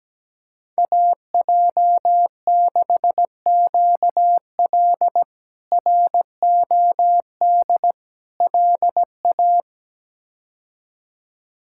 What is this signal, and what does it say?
Morse code, heard as a keyed tone, about 17 wpm, 700 Hz. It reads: AJ6QL ROD LA
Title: K1USN SST QSO Practice - 17wpm